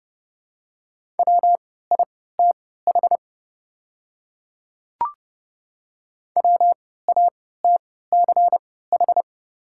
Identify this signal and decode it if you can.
Morse code, heard as a keyed tone, about 30 wpm, spaced as for 17 wpm, 700 Hz. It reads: WITH WATCH
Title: 3x Character Spacing - Top 300 Words - 30wpm